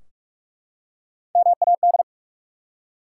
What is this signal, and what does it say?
Morse code, heard as a keyed tone, about 44 wpm, 700 Hz. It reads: MAD
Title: Three Letter Words - 45wpm